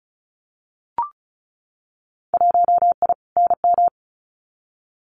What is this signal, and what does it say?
Morse code, heard as a keyed tone, about 34 wpm, 700 Hz. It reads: 1I NM
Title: Winter Field Day Exchanges Encore 35wpm